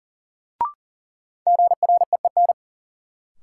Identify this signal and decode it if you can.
Morse code, heard as a keyed tone, about 40 wpm, 700 Hz. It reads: GREEN
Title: Sentences from Top 500 Words - New Word Drill - 40wpm